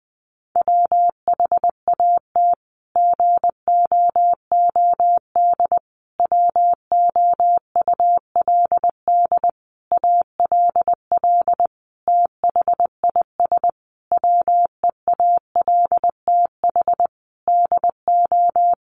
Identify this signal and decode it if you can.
Morse code, heard as a keyed tone, about 20 wpm, 700 Hz. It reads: WHAT GOOD WOULD ALL THIS WEALTH DO
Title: Sentences from Top 1200 Words 20wpm